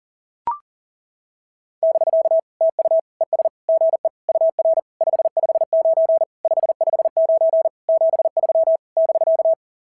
Morse code, heard as a keyed tone, about 39 wpm, 660 Hz. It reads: <BK> TU ES GE UR 559 559 73 <BK>
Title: SOTA QSO Practice 40wpm